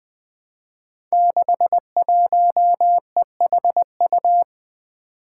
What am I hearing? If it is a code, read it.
Morse code, heard as a keyed tone, about 20 wpm, 700 Hz. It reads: 61EHU
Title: Five Challenging Letter-Number Combinations 20wpm